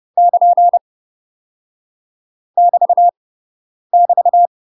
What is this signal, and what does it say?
Morse code, heard as a keyed tone, about 30 wpm, 700 Hz. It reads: <KN> <BT> <BT>
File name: UK MASTER QSOs TRAINING / 30WPM / UK MASTER QSOs 0034